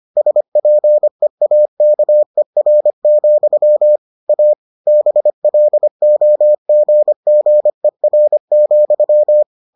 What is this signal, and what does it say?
Morse code, heard as a keyed tone, about 25 wpm, 600 Hz. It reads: SPEAKER, A BLOGGER,